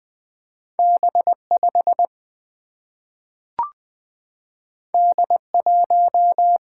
Morse code, heard as a keyed tone, about 20 wpm, 700 Hz. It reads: B 5 D 1
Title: Single-Letter Single-Number 20wpm